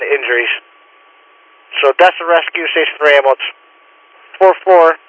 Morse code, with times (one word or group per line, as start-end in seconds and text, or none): none